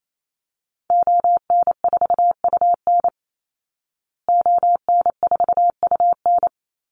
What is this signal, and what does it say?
Morse code, heard as a keyed tone, about 28 wpm, 700 Hz. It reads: ON4UN ON4UN
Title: Call Signs Repeated 3x - 28wpm